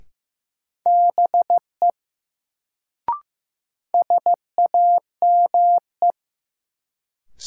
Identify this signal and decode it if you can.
Morse code, heard as a keyed tone, about 15 wpm, 700 Hz. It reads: BE SAME
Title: Sets of 2 Words - New Word Drill - 15wpm